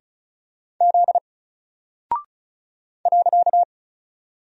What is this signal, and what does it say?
Morse code, heard as a keyed tone, about 36 wpm, 700 Hz. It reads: Z .
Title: Single Character (Letter:Number:Punctuation:Prosign) Alt 35wpm